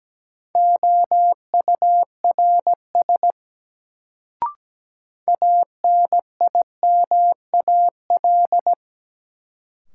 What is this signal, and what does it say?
Morse code, heard as a keyed tone, about 17 wpm, 700 Hz. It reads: OURS ANIMAL
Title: Sentences from Top 1300 Words - New Word Drill Alt - 17wpm